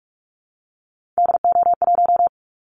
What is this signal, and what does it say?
Morse code, heard as a keyed tone, about 44 wpm, 700 Hz. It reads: DO1
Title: Callsign Prefixes for Top 25 DXCC - Repeated 3x - 45wpm